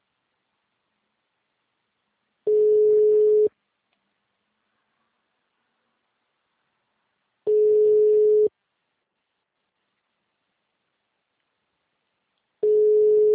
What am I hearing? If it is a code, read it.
Ringback tone (Europe, 425 Hz).